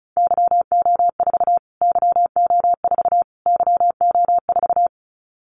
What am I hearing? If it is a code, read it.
Morse code, heard as a keyed tone, about 34 wpm, 700 Hz. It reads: YO4 YO4 YO4
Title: Callsign Prefixes for Top 25 DXCC - Repeated 3x - 35wpm